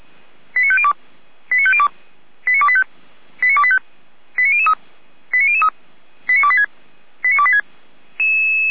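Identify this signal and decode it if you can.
Five-tone selective call (ZVEI), heard as quick runs of five tones, tones of about 70 ms. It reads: 89502 89502 89285 89285 89003 89003 89286 89286
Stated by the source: ZVEI five-tone selective call 89220 received on SDR, from feuerwehr-orlamuende (Orlamünde fire brigade) audio server.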